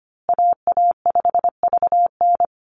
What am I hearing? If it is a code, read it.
Morse code, heard as a keyed tone, about 25 wpm, 700 Hz. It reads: AA5VN
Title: USA Call Signs Repeated 3x 25wpm